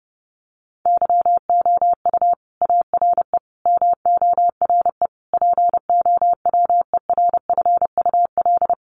Morse code, heard as a keyed tone, about 30 wpm, 700 Hz. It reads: YOU ARE MORE POWERFUL
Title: Sentences from Top 1300 Words 30wpm